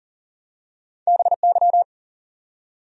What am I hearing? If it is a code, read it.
Morse code, heard as a keyed tone, about 40 wpm, 700 Hz. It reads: BY
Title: Top 100 Words 40wpm